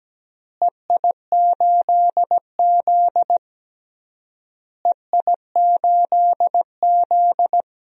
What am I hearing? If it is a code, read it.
Morse code, heard as a keyed tone, about 17 wpm, 700 Hz. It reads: EI8Z EI8Z
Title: DXCC-100 Call Signs - 4 Characters - Repeated 3x - 17wpm